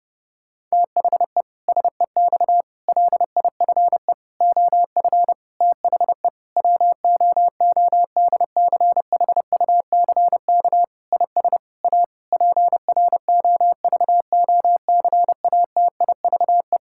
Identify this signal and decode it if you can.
Morse code, heard as a keyed tone, about 30 wpm, 700 Hz. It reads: THE SEX LIFE OF THE WOODCHUCK IS A PROVOCATIVE
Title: Pangram Challenge v2 - 30wpm